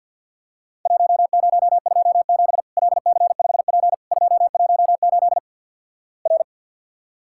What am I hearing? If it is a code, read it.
Morse code, heard as a keyed tone, about 48 wpm, 700 Hz. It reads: 1027 LC5P 218 R
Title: Worked All Europe DX Contest - QTC Exchange - 50wpm